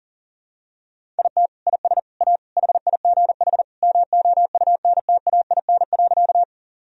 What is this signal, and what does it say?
Morse code, heard as a keyed tone, about 40 wpm, 700 Hz. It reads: IT IS A HIGH MOUNTAIN.
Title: Sentences from Top 1000 Words - Review Part 1 - 40wpm